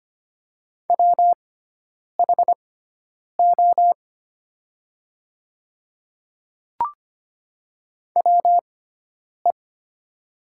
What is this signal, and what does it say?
Morse code, heard as a keyed tone, about 25 wpm, spaced as for 9 wpm, 700 Hz. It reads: WHO WE
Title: 6x Character Spacing - Top 100 Words - 25wpm